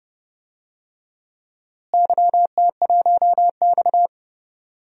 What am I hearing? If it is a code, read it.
Morse code, heard as a keyed tone, about 30 wpm, 700 Hz. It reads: YT1X
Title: DXCC-100 Call Signs - 4 Characters - Repeated 3x - 30wpm